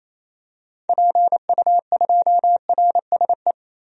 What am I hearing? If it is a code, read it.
Morse code, heard as a keyed tone, about 27 wpm, 700 Hz. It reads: PU2RSE
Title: DXCC-100 Call Signs - 6 Characters - 28wpm